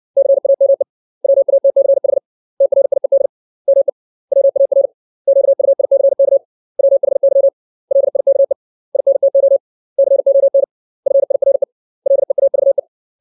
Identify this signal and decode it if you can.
Morse code, heard as a keyed tone, about 60 wpm, 550 Hz. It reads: CAME WATCH TRIED ME WAR QUICK WHY LIKE INTO CON FIRE LEAVE
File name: 632 words / 632 Words 60WPM